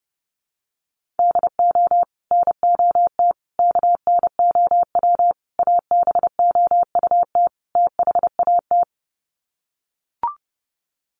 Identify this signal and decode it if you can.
Morse code, heard as a keyed tone, about 30 wpm, 700 Hz. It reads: DO NOT KNOW ABOUT THAT
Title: Sets of 5 Words Encore2 30wpm